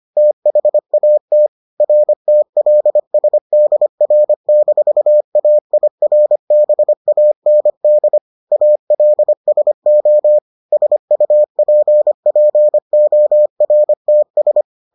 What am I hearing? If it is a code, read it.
Morse code, heard as a keyed tone, about 25 wpm, 600 Hz. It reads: THAT RTLSDR-AIRBAND ALSO SUPPORTS